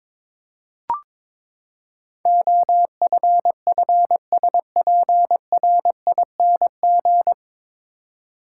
Morse code, heard as a keyed tone, about 21 wpm, 700 Hz. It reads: OFFSPRING
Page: Morse Code Ninja practice 22wpm